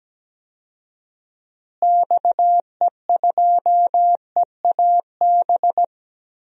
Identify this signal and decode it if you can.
Morse code, heard as a keyed tone, about 17 wpm, 700 Hz. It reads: XE2EAB
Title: DXCC-100 Call Signs - 6 Characters - Repeated 3x - 17wpm